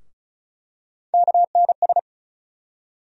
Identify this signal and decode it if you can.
Morse code, heard as a keyed tone, about 34 wpm, 700 Hz. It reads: KNS
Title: USA Call Sign Postfixes - 3 Letters 35wpm